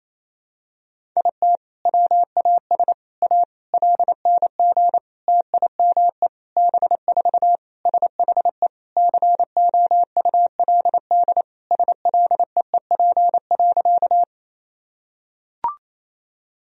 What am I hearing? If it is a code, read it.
Morse code, heard as a keyed tone, about 28 wpm, 700 Hz. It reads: IT WAS A LNG TIME B4 SHE COULD SLEEP.
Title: Sentences from Top 500 Words and Abbreviations 28wpm